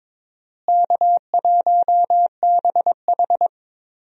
Morse code, heard as a keyed tone, about 22 wpm, 700 Hz. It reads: K1BH
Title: USA Call Signs - 4 Characters - Repeated 2x - 22wpm